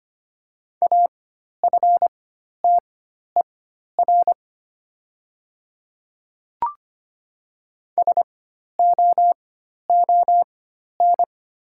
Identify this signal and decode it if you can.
Morse code, heard as a keyed tone, about 25 wpm, spaced as for 12 wpm, 690 Hz. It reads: AFTER SOON
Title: 4x Character Spacing - Top 500 Words - 25wpm